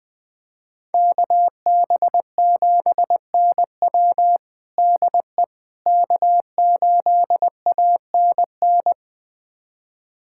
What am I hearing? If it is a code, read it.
Morse code, heard as a keyed tone, about 20 wpm, 700 Hz. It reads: KB7NW DE K8ANN
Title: USA Callsign DE Callsign - 5 Characters - 20wpm